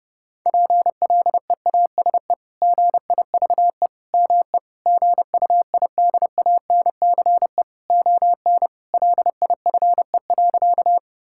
Morse code, heard as a keyed tone, about 32 wpm, 700 Hz. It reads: PLEASE GIVE ME GUIDANCE ON LIFE.
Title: Sentences from Top 1600 Words 30wpm